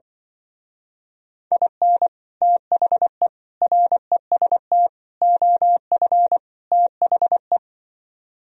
Morse code, heard as a keyed tone, about 24 wpm, 700 Hz. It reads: IN THE REST OF THE
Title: Speed-Racing - Sets of 5 Words 20wpm